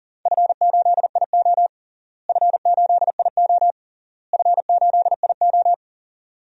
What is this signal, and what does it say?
Morse code, heard as a keyed tone, about 40 wpm, 700 Hz. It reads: F8IO F8IO F8IO
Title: DXCC-100 Call Signs - 4 Characters - Repeated 3x - 40wpm